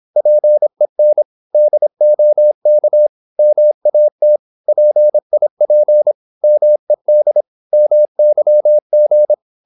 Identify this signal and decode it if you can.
Morse code, heard as a keyed tone, about 26 wpm, 600 Hz. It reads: PEN DOK MAT PIP MED MYG
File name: Korte ord DK 26wpm